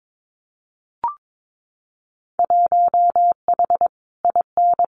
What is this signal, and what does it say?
Morse code, heard as a keyed tone, about 22 wpm, 700 Hz. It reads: 1H IN
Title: Winter Field Day Exchanges 22wpm